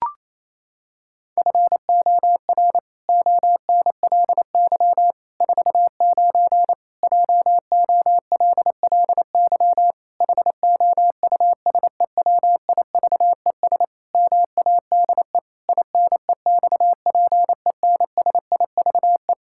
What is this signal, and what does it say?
Morse code, heard as a keyed tone, about 28 wpm, 700 Hz. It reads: FOR ONLY 49 JOLLY HOUSEWIVES MADE INEXPENSIVE
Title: Pangram Challenge v2 - 28wpm